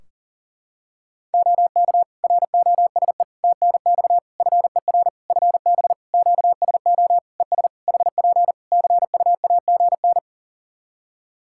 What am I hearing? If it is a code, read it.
Morse code, heard as a keyed tone, about 40 wpm, 700 Hz. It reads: OK ROSE TNX FER FB QSO ES HP CUAGN
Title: Practice QSO - Exchange 4 - Ending - 40wpm